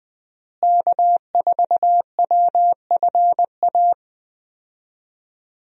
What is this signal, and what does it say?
Morse code, heard as a keyed tone, about 20 wpm, 700 Hz. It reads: K4WFA
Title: USA Call Signs - 5 Characters - Repeated 2x - 20wpm